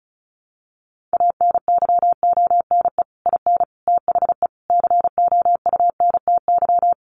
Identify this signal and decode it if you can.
Morse code, heard as a keyed tone, about 34 wpm, 700 Hz. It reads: ANYONE IN THE COUNTY
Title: Sentences from Top 1200 Words 35wpm